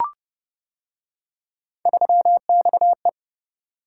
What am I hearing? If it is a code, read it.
Morse code, heard as a keyed tone, about 28 wpm, 700 Hz. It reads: 3XE